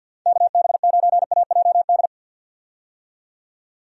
Morse code, heard as a keyed tone, about 47 wpm, 700 Hz. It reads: KB9AJD
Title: USA Call Signs - 6 Characters - 50wpm